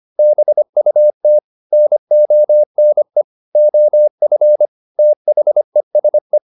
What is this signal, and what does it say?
Morse code, heard as a keyed tone, about 25 wpm, 600 Hz. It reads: BUT NONE OF THESE